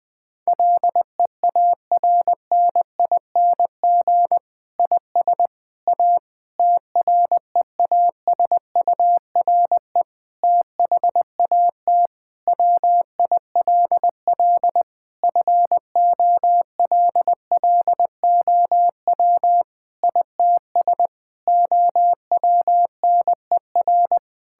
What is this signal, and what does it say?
Morse code, heard as a keyed tone, about 21 wpm, 700 Hz. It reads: LEARNING IS A TREASURE THAT WILL FOLLOW ITS OWNER
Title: Difficult Sentences with Hints - 20wpm